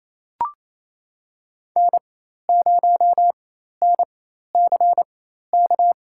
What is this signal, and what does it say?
Morse code, heard as a keyed tone, about 28 wpm, spaced as for 13 wpm, 700 Hz. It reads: N0NCK
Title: USA Call Signs - 5 Characters - 4x Character Spacing - 28wpm